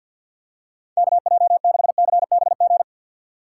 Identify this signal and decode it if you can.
Morse code, heard as a keyed tone, about 52 wpm, 700 Hz. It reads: KJ6CBG
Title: USA Call Signs - 6x Character Spacing - 50wpm